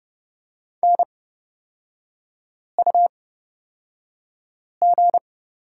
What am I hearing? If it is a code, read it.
Morse code, heard as a keyed tone, about 31 wpm, spaced as for 5 wpm, 700 Hz. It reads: NUG
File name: Lesson 38e - Letter q - Three Characters 30wpm